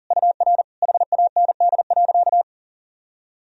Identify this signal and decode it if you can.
Morse code, heard as a keyed tone, about 40 wpm, 700 Hz. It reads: UR HAND.
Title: Sentences from Top 400 Words and Abbreviations 40wpm